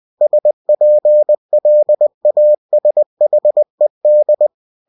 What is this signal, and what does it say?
Morse code, heard as a keyed tone, about 20 wpm, 600 Hz. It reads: SPLASHED
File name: cw-books / winnie-the-pooh / winnie-20wpm-page-0044